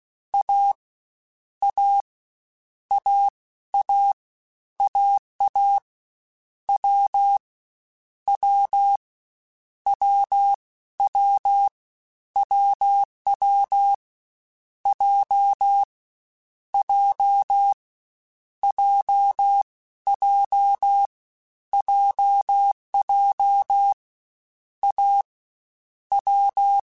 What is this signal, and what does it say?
Morse code, heard as a keyed tone, about 16 wpm, spaced as for 13 wpm, 800 Hz. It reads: A A AA AA W W WW WW J J JJ JJ A W
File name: Difficult Combos / Difficult Combos-mp3s / AWJ ShortRun @15